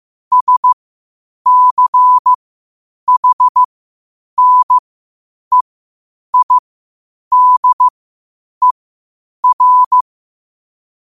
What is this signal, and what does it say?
Morse code, heard as a keyed tone, about 15 wpm, spaced as for 8 wpm, 1000 Hz. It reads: SCHNEIDER
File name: qotdfiles / 8wpm